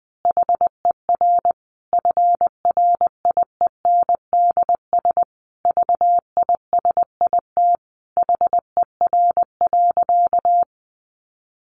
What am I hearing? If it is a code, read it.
Morse code, heard as a keyed tone, about 20 wpm, 700 Hz. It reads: HER FRIENDS VISIT HER.
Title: Easy Phrases 20wpm